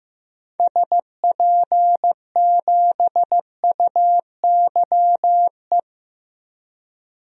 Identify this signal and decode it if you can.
Morse code, heard as a keyed tone, about 15 wpm, 690 Hz. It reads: SP7UYE
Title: DXCC-100 Call Signs - 6 Characters - 15wpm